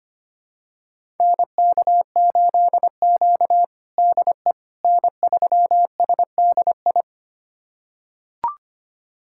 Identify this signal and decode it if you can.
Morse code, heard as a keyed tone, about 25 wpm, 700 Hz. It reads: NK8Q DE N3SDI
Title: USA Callsign DE Callsign 25wpm